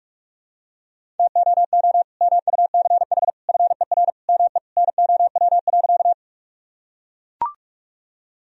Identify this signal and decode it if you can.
Morse code, heard as a keyed tone, about 44 wpm, 700 Hz. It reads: TOO MUCH FER ME NOW.
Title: Sentences from Top 200 Words and Abbreviations 45wpm